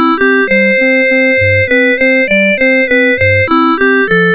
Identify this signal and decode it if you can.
There is no code.